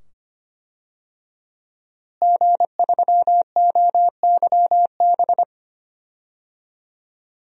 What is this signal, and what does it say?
Morse code, heard as a keyed tone, about 25 wpm, 700 Hz. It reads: G3OYB
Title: DXCC-100 Call Signs - 5 Characters - Repeated 2x - 25wpm